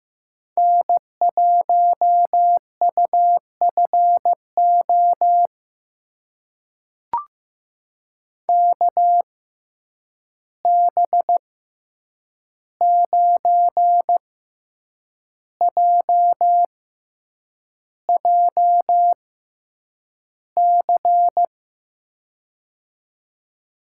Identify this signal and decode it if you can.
Morse code, heard as a keyed tone, about 15 wpm, spaced as for 7 wpm, 700 Hz. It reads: N1UFO KB9JJC
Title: USA Call Signs - 6x Character Spacing - 15wpm